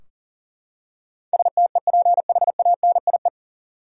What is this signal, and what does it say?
Morse code, heard as a keyed tone, about 40 wpm, 700 Hz. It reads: STEPHANIE
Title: USA Common Names - 40wpm